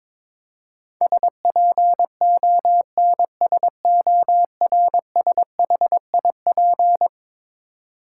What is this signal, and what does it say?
Morse code, heard as a keyed tone, about 22 wpm, 700 Hz. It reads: SPONSORSHIP
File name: practice / 22wpm / Eleven Letter Words - 22wpm